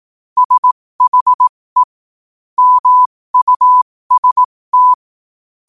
Morse code, heard as a keyed tone, about 18 wpm, spaced as for 15 wpm, 1000 Hz. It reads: SHE MUST
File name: qotdfiles / 15wpm